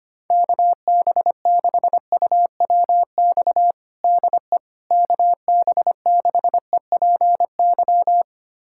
Morse code, heard as a keyed tone, about 25 wpm, 700 Hz. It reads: KB6UWX DE KB6EPY
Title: USA Callsign DE Callsign 25wpm